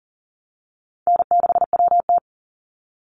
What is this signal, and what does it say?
Morse code, heard as a keyed tone, about 40 wpm, 700 Hz. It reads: N6WT
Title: NA SOTA Chaser Callsigns 40wpm